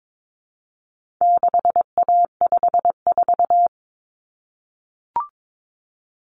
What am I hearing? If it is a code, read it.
Morse code, heard as a keyed tone, about 22 wpm, 700 Hz. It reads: 6A54
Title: Common Cut Numbers - 4-Digit 22wpm